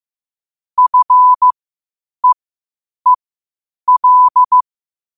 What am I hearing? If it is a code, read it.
Morse code, heard as a keyed tone, about 15 wpm, spaced as for 8 wpm, 1000 Hz. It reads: FEEL
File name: qotdfiles / 8wpm